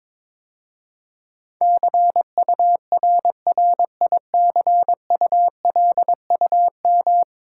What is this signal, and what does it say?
Morse code, heard as a keyed tone, about 22 wpm, 700 Hz. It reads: CURRICULUM
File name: practice / 22wpm / 10 Letter Words - First 3 Letters Given - 22wpm